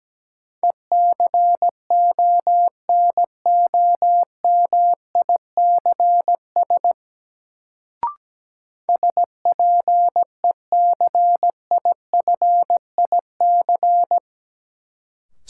Morse code, heard as a keyed tone, about 17 wpm, 700 Hz. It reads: ECONOMICS SPECIFIC
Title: Sentences from Top 1000 Words - Review Part 3 - New Word Drill - 17wpm